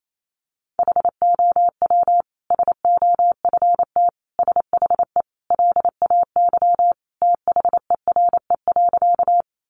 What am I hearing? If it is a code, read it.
Morse code, heard as a keyed tone, about 28 wpm, 700 Hz. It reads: HOW SOFT SHE LAY THERE.
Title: Sentences from Top 1500 Words 28wpm